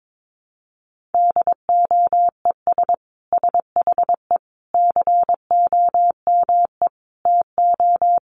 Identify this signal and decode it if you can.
Morse code, heard as a keyed tone, about 22 wpm, 700 Hz. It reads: DOES SHE COME TO
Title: Sentences from Top 1500 Words 22wpm